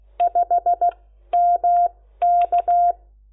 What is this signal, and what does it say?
Morse code, heard as a keyed tone, about 16 wpm, spaced as for 13 wpm, 700 Hz. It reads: 5 MK
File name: Lecciones / CLASE Nº10